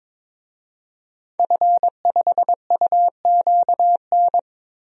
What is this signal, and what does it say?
Morse code, heard as a keyed tone, about 22 wpm, 700 Hz. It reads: F5UQN
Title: DXCC-100 Call Signs - 5 Characters - Repeated 3x - 22wpm